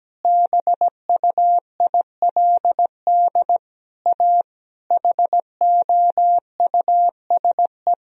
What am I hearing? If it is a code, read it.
Morse code, heard as a keyed tone, about 17 wpm, 700 Hz. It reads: BUILD A HOUSE
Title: Sentences from Top 1000 Words - Review Part 1 - 17wpm